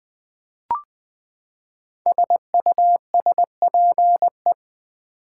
Morse code, heard as a keyed tone, about 20 wpm, 700 Hz. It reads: SUSPE
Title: Build A Word - 5 Letters At A Time - 10-Letter Words - 20wpm